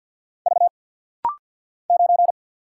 Morse code, heard as a keyed tone, about 50 wpm, 700 Hz. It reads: V 9
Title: Single Letter-Number Alt - Rapid-Fire 50wpm